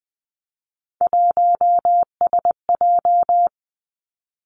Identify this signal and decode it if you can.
Morse code, heard as a keyed tone, about 19 wpm, 700 Hz. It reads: 1SJ
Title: Three Challenging Letter-Number Combinations 20wpm